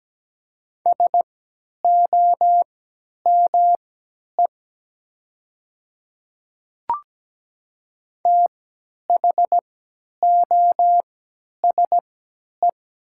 Morse code, heard as a keyed tone, about 17 wpm, spaced as for 10 wpm, 700 Hz. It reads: SOME THOSE
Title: 3x Character Spacing - Top 500 Words - 17wpm